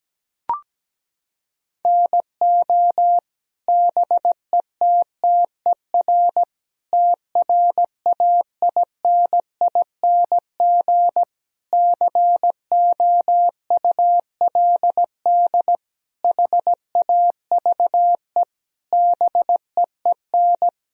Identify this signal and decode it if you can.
Morse code, heard as a keyed tone, about 17 wpm, 700 Hz. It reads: NO BETTER TRAINING COULD HAVE BEEN
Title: Sentences from Top 1000 Words - Review Part 4 - 17wpm